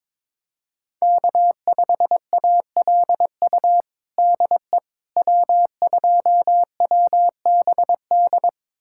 Morse code, heard as a keyed tone, about 21 wpm, 700 Hz. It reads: K5ALU DE W2WBD
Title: USA Callsign DE Callsign - 5 Characters - 22wpm